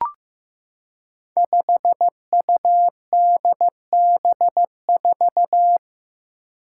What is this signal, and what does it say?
Morse code, heard as a keyed tone, about 15 wpm, 700 Hz. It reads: 5UDB4
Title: Five Challenging Letter-Number Combinations 15wpm